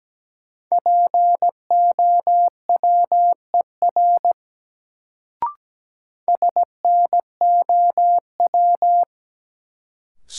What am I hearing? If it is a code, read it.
Morse code, heard as a keyed tone, about 17 wpm, 700 Hz. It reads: POWER SNOW
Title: Sentences from Top 500 Words - New Word Drill - 17wpm